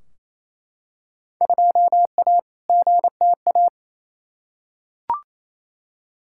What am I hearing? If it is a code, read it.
Morse code, heard as a keyed tone, about 28 wpm, 700 Hz. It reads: 2A GTA
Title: Field Day Exchanges Encore - 28wpm